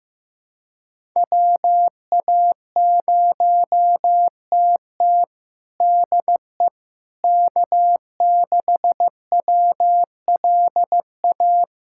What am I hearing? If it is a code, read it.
Morse code, heard as a keyed tone, about 15 wpm, 700 Hz. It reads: WA0TT DE K6WLA
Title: USA Callsign DE Callsign - 5 Characters - 15wpm